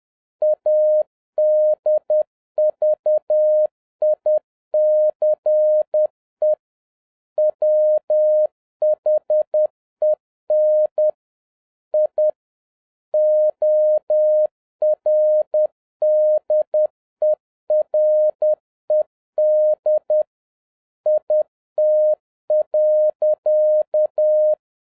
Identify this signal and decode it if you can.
Morse code, heard as a keyed tone, about 10 wpm, 610 Hz. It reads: ADVICE WHEN I ORDERED IT.